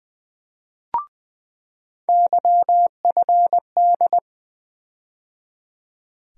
Morse code, heard as a keyed tone, about 20 wpm, 700 Hz. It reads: YFD
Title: USA Call Sign Postfixes - 3 Letters 20wpm